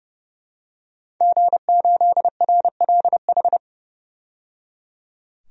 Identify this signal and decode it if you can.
Morse code, heard as a keyed tone, about 30 wpm, 700 Hz. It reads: G8RLH
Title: DXCC-100 Call Signs - 5 Characters - 30wpm